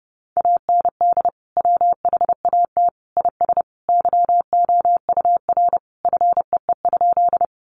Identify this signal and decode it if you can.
Morse code, heard as a keyed tone, about 30 wpm, 700 Hz. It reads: AND WHAT IS YOUR FEE?
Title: Sentences from Top 1500 Words 30wpm